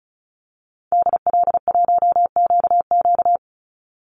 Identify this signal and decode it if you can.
Morse code, heard as a keyed tone, about 34 wpm, 700 Hz. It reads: DL1QQ
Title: Call Signs - 35wpm